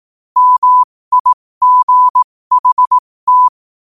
Morse code, heard as a keyed tone, about 18 wpm, spaced as for 15 wpm, 1000 Hz. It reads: MIGHT
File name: qotdfiles / 15wpm